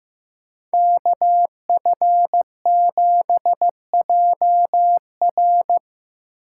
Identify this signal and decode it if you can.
Morse code, heard as a keyed tone, about 15 wpm, 700 Hz. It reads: KF7JR
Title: USA Call Signs - 5 Characters - 3x Character Spacing - 15wpm